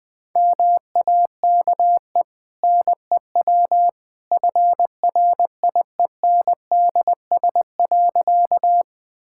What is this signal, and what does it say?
Morse code, heard as a keyed tone, about 20 wpm, 700 Hz. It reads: MAKE NEW FRIENDS.
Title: Easy Sentences with Hints - 20wpm